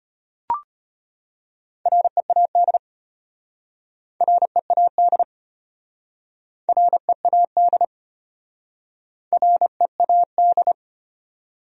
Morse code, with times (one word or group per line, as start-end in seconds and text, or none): 1.9-2.8 READ
4.2-5.2 READ
6.7-7.8 READ
9.3-10.7 READ